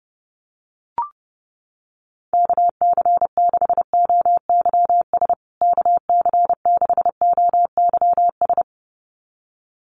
Morse code, heard as a keyed tone, about 30 wpm, 700 Hz. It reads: KC6OYS KC6OYS
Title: USA Call Signs Repeated 2x 30wpm